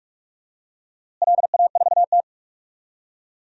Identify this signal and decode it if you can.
Morse code, heard as a keyed tone, about 44 wpm, 700 Hz. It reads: LA4T
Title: DXCC-100 Call Signs - 4 Characters - Repeated 2x - 45wpm